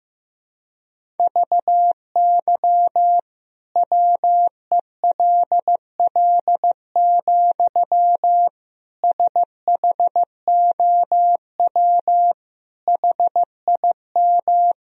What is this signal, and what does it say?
Morse code, heard as a keyed tone, about 15 wpm, 700 Hz. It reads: VY WELL, SHOW HIM
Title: Sentences from Top 200 Words and Abbreviations 15wpm